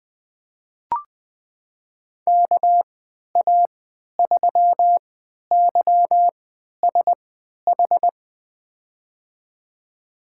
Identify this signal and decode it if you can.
Morse code, heard as a keyed tone, about 20 wpm, spaced as for 11 wpm, 700 Hz. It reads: KA3YSH
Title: USA Call Signs - 6 Characters - 3x Character Spacing - 20wpm